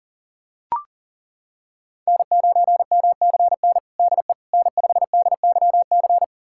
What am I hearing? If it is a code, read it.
Morse code, heard as a keyed tone, about 40 wpm, 700 Hz. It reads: N9MCN DE N5DYC
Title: USA Callsign DE Callsign - 5 Characters - 40wpm